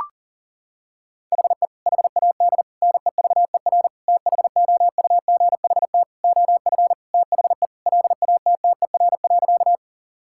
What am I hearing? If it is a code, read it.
Morse code, heard as a keyed tone, about 37 wpm, 700 Hz. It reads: HE HAD NEVER THOUGHT OF THE LATTER.
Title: Sentences from Top 1600 Words 40wpm